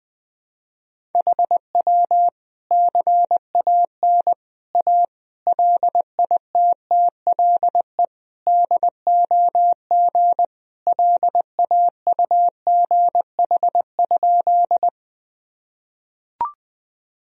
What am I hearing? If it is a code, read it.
Morse code, heard as a keyed tone, about 20 wpm, 700 Hz. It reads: HW CAN A LITTLE DOG LAUGH?
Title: Sentences from Top 500 Words and Abbreviations 20wpm